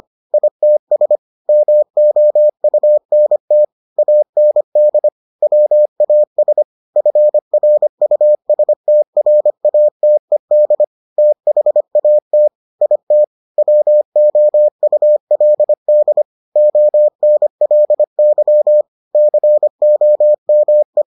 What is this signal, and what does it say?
Morse code, heard as a keyed tone, about 25 wpm, 600 Hz. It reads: ITS MOUNT AND WAS FRUSTRATED THAT IT WOULD ONLY COME